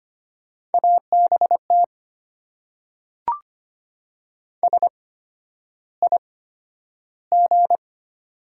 Morse code, heard as a keyed tone, about 25 wpm, spaced as for 11 wpm, 700 Hz. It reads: ABT SIG
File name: Top 90 QSO Elements - 8x Character Spacing - 25wpm